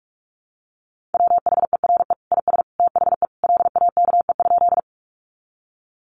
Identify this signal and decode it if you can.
Morse code, heard as a keyed tone, about 44 wpm, 700 Hz. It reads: WHERE IS THE LAKE?
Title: Sentences from Top 1100 Words 45wpm